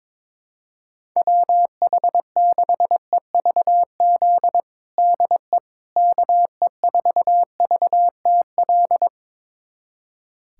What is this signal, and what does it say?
Morse code, heard as a keyed tone, about 22 wpm, 700 Hz. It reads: WH6EVZ DE KE4VTL
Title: USA Callsign DE Callsign - 6 Characters - 22wpm